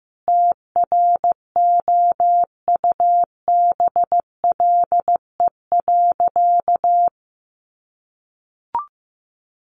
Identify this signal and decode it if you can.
Morse code, heard as a keyed tone, about 15 wpm, 700 Hz. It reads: TROUBLE.
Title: Sentences from Top 1500 Words 15wpm